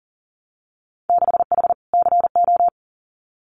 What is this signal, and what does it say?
Morse code, heard as a keyed tone, about 42 wpm, 700 Hz. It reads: 6H CO
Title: Winter Field Day Exchanges Encore 40wpm